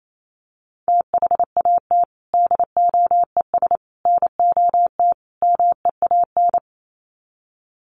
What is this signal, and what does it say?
Morse code, heard as a keyed tone, about 28 wpm, 700 Hz. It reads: THAT DOES NOT MEAN
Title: Sets of 4 Words 28wpm